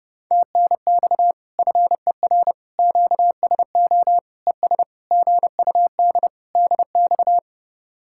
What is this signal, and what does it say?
Morse code, heard as a keyed tone, about 30 wpm, 700 Hz. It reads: TNX FER QSO ES GUD DX